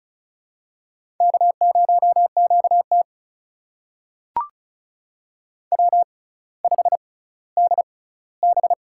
Morse code, heard as a keyed tone, about 36 wpm, spaced as for 18 wpm, 700 Hz. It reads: K0QT W5DB
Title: USA Call Signs - 4 Characters - 6x Character Spacing - 35wpm